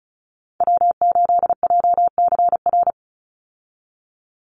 Morse code, heard as a keyed tone, about 34 wpm, 700 Hz. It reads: W8JCR
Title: USA Call Signs 35wpm